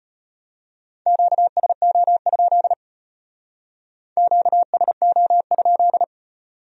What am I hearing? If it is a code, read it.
Morse code, heard as a keyed tone, about 38 wpm, 700 Hz. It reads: QSO? QSO?